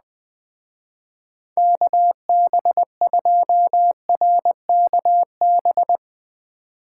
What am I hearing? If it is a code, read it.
Morse code, heard as a keyed tone, about 20 wpm, 700 Hz. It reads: KB2RKB
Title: Speed Racing - USA 2x3 Call Signs 17wpm